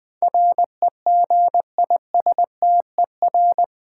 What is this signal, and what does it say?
Morse code, heard as a keyed tone, about 20 wpm, 700 Hz. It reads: REGISTER
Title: Sentences from Top 1200 Words - New Word Drill - 20wpm